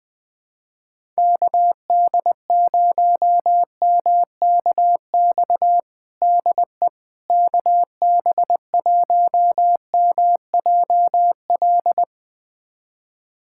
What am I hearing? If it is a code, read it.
Morse code, heard as a keyed tone, about 20 wpm, 700 Hz. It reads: KD0MKX DE KB1MJL
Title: USA Callsign DE Callsign - 6 Characters - 20wpm